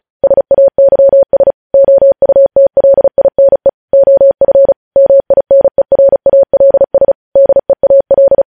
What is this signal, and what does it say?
Morse code, heard as a keyed tone, about 36 wpm, 550 Hz. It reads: SAYS OUTLINE OF MINERALS DEAL
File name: podcast / News headlines-1x-35x35